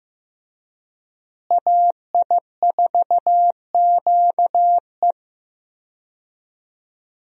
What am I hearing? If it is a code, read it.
Morse code, heard as a keyed tone, about 15 wpm, 700 Hz. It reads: AI4QE
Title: USA Call Signs - 5 Characters - Repeated 3x - 15wpm